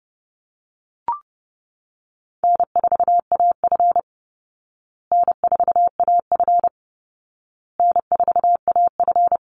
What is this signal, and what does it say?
Morse code, heard as a keyed tone, about 30 wpm, 700 Hz. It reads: N4AF N4AF N4AF
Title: Easy Call Signs Repeated 3x - 30wpm